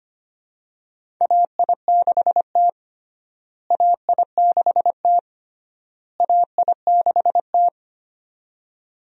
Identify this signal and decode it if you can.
Morse code, heard as a keyed tone, about 25 wpm, 700 Hz. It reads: AI6T AI6T AI6T
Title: USA Call Signs - 4 Characters - Repeated 3x - 25wpm